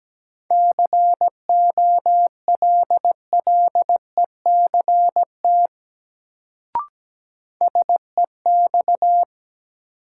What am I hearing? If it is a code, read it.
Morse code, heard as a keyed tone, about 17 wpm, 700 Hz. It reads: COLLECT SEX
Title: Sentences from Top 1600 Words - New Word Drill - 17wpm